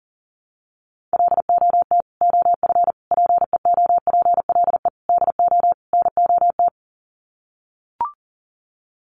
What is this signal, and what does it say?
Morse code, heard as a keyed tone, about 40 wpm, 700 Hz. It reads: LOT OF PEOPLE DO NOT
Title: Sets of 5 Words Encore 40wpm